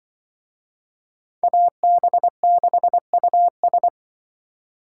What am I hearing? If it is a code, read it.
Morse code, heard as a keyed tone, about 24 wpm, 700 Hz. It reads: AB6US
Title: Speed Racing - USA 2x2 Call Signs 20wpm